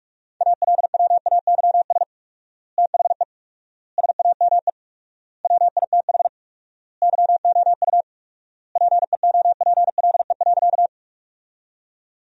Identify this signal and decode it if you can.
Morse code, heard as a keyed tone, about 46 wpm, 700 Hz. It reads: ALWAYS THE SAME WITH YOU PEOPLE.
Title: 4x Word-Spacing - Sentences from Top 300 Words - 45wpm